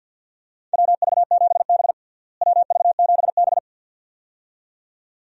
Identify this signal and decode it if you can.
Morse code, heard as a keyed tone, about 51 wpm, 700 Hz. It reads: WV7B WV7B
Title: USA Call Signs - 4 Characters - Repeated 2x - 50wpm